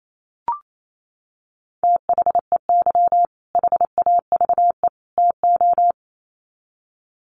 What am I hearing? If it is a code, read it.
Morse code, heard as a keyed tone, about 28 wpm, 700 Hz. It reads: THEY HAVE TO
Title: Sets of 3 Words 28wpm